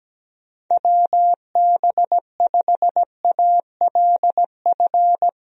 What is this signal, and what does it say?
Morse code, heard as a keyed tone, about 17 wpm, 700 Hz. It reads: WB5ALF
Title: USA Call Signs - 6 Characters - 17wpm